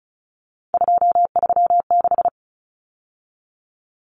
Morse code, heard as a keyed tone, about 34 wpm, 700 Hz. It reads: 236
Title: Numbers - 3-Digit 35wpm